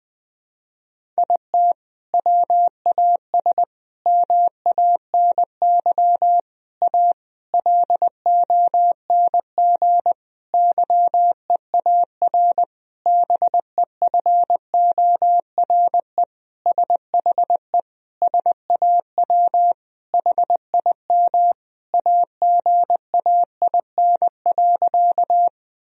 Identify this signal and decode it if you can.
Morse code, heard as a keyed tone, about 20 wpm, 700 Hz. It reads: IT WAS MANY A LONG YEAR BEFORE SHE SAW HIM AGAIN.
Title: Sentences from Top 1000 Words - Review Part 1 - 20wpm